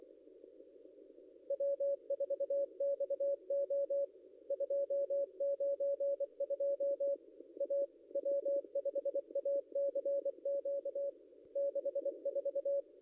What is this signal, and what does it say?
Morse code, heard as a keyed tone, about 23 wpm, 560 Hz. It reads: W4XO 292 A W5ACQ 64
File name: contests / ss-cw-k5tr-op-n5rz -0800